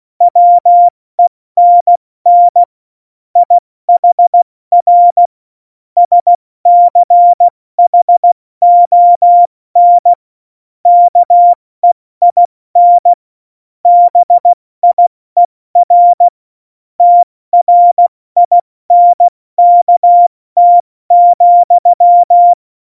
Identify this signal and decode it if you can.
Morse code, heard as a keyed tone, about 16 wpm, spaced as for 14 wpm, 700 Hz. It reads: WENN IHR SCHON KEIN BIER TRINKT,
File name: Morsekurs / Witze / Witz 08. Oktober 2019, 70 F80